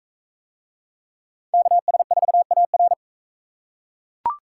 Morse code, heard as a keyed tone, about 38 wpm, 700 Hz. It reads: KS4AR